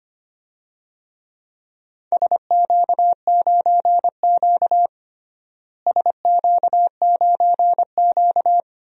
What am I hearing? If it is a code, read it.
Morse code, heard as a keyed tone, about 25 wpm, 700 Hz. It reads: SQ9Q SQ9Q
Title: DXCC-100 Call Signs - 4 Characters - Repeated 2x - 25wpm